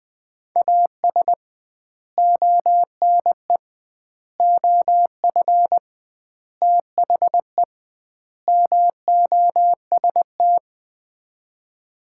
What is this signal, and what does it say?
Morse code, heard as a keyed tone, about 20 wpm, 700 Hz. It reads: AS ONE OF THE MOST
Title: 2x Word-Spacing - Sets of 5 Words - 20wpm